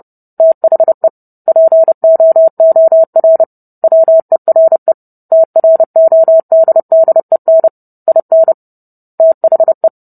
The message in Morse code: THE POOR WERE TRODDEN IN THE

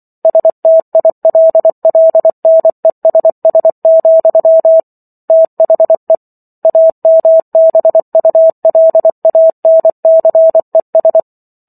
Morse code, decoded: STILLNESS, THE AMBULANCES

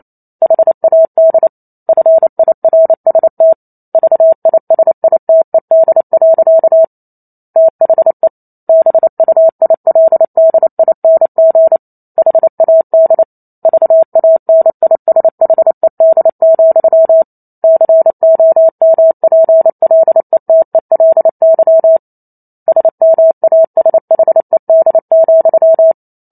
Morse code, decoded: HAD FIRST VISITED. THE BUILDING HAD VANISHED, COMPLETELY SMASHED,

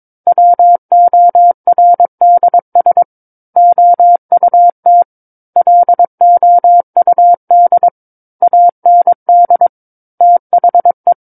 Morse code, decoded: WORDS OUT LOUD AND THE

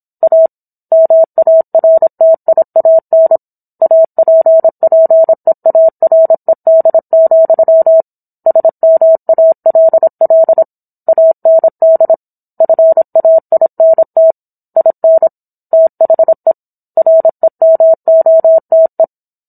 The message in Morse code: A MARTIAN APPEARED, SMALL AND FAINT IN THE REMOTE

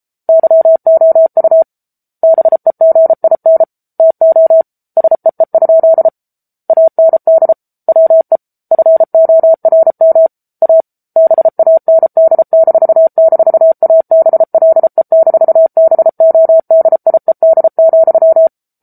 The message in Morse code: YOU BEGIN TO SEE? AND WE FORM A BAND--ABLE-BODIED,

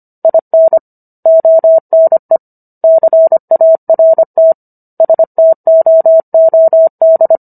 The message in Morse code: IN ONE CART STOOD

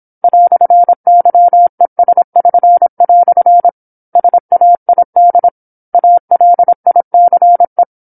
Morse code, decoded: "YES<SN>" SAID ALICE

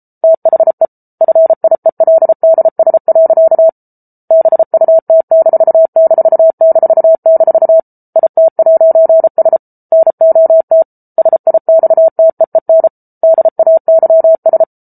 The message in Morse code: THE FIELDS. BUT---- IT'S NOT SIXTEEN DAYS

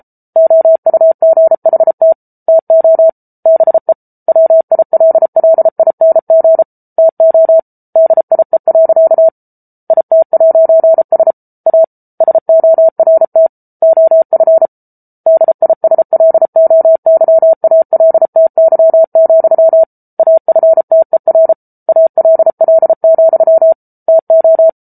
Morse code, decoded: OUGHT TO BE WILLING TO DIE. IT'S A SORT OF DISLOYALTY, AFTER ALL, TO

